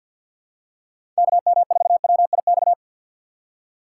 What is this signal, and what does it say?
Morse code, heard as a keyed tone, about 46 wpm, 700 Hz. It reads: KM4WIX
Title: USA Call Signs - 6x Character Spacing - 50wpm